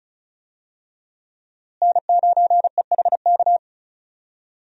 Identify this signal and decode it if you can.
Morse code, heard as a keyed tone, about 35 wpm, 700 Hz. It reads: N9EHK
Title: USA Call Signs - 5 Characters - Repeated 3x - 35wpm